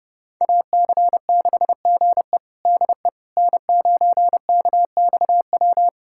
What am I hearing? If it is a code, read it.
Morse code, heard as a keyed tone, about 30 wpm, 700 Hz. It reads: AC6GE DE N9KXW
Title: USA Callsign DE Callsign - 5 Characters - 30wpm